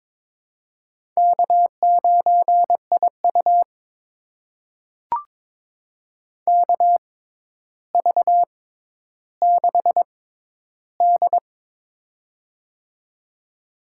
Morse code, heard as a keyed tone, about 22 wpm, spaced as for 11 wpm, 700 Hz. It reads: K9IU KV6D
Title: USA Call Signs - 4 Characters - 6x Character Spacing - 22wpm